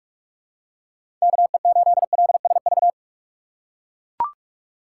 Morse code, heard as a keyed tone, about 41 wpm, 700 Hz. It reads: KE8LSV
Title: USA Call Signs - 6 Characters - 2x Character Spacing - 45wpm